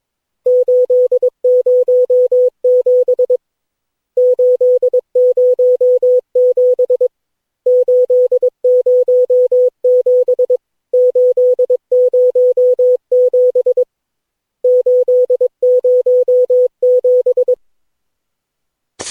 Morse code, heard as a keyed tone, about 23 wpm, 500 Hz. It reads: 807 807 807 807 807
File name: cw / Session9 cw training